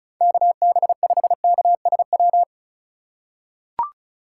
Morse code, heard as a keyed tone, about 36 wpm, 700 Hz. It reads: KB5KSW
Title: USA Call Signs - 6 Characters - 8x Character Spacing - 35wpm